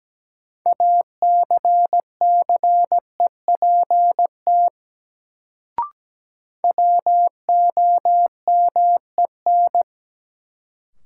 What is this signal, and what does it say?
Morse code, heard as a keyed tone, about 17 wpm, 700 Hz. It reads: ACCEPT WOMEN
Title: Sentences from Top 1300 Words - New Word Drill - 17wpm